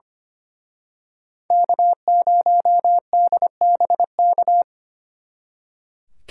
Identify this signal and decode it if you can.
Morse code, heard as a keyed tone, about 31 wpm, 700 Hz. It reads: K0DBK
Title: Speed Racing - USA 1x3 Call Signs 25wpm